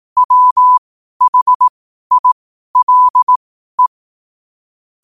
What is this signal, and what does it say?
Morse code, heard as a keyed tone, about 18 wpm, spaced as for 12 wpm, 1000 Hz. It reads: WHILE
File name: qotdfiles / 12wpm